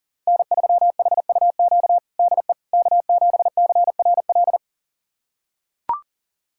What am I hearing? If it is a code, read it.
Morse code, heard as a keyed tone, about 40 wpm, 700 Hz. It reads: N3HUQ DE K7CRL
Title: USA Callsign DE Callsign - 5 Characters - 40wpm